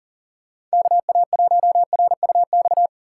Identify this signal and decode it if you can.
Morse code, heard as a keyed tone, about 40 wpm, 700 Hz. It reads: KA1RUX
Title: USA Call Signs - 6 Characters - 6x Character Spacing - 40wpm